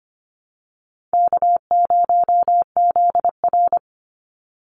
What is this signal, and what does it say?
Morse code, heard as a keyed tone, about 25 wpm, 700 Hz. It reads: K0ZR
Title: Easy Call Signs - 25wpm